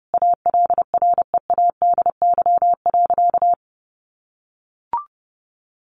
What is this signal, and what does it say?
Morse code, heard as a keyed tone, about 30 wpm, 700 Hz. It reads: ALREADY.
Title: Sentences from Top 1000 Words 30wpm